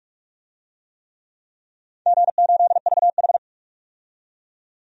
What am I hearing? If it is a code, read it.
Morse code, heard as a keyed tone, about 45 wpm, 700 Hz. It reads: G8VH
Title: DXCC-100 Call Signs - 4 Characters - 45wpm